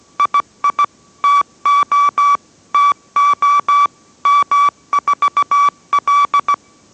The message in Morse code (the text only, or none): II TO TO M4L